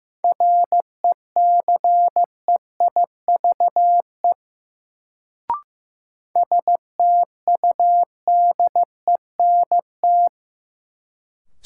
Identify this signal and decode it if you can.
Morse code, heard as a keyed tone, about 15 wpm, 700 Hz. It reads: RECEIVE STUDENT
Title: Sentences from Top 1000 Words - New Word Drill - 15wpm